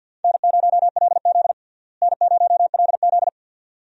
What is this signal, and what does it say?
Morse code, heard as a keyed tone, about 52 wpm, 700 Hz. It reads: N0LZ N0LZ
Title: USA Call Signs - 4 Characters - Repeated 2x - 50wpm